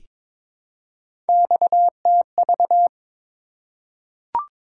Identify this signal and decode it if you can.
Morse code, heard as a keyed tone, about 22 wpm, 700 Hz. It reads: XTV
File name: practice / 22wpm / Three Letter-Number Combinations - 22wpm